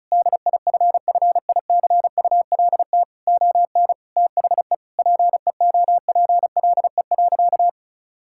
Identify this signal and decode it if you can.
Morse code, heard as a keyed tone, about 34 wpm, 700 Hz. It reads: DIFFICULT ON THE PEOPLE.
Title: Sentences from Top 1000 Words - Review Part 5 - 35wpm